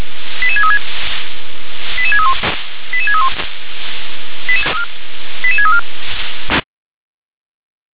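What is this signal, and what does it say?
Five-tone selective call (ZVEI), heard as quick runs of five tones, tones of about 70 ms. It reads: 88636 88631 88631 88634
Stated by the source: ZVEI five-tone selective call 88633, received on SDR.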